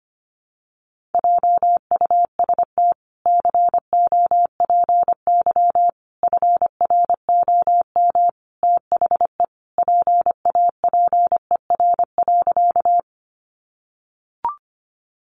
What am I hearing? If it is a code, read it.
Morse code, heard as a keyed tone, about 25 wpm, 700 Hz. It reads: JUST COPY FROM THE PAPER.
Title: Sentences from Top 1400 Words 25wpm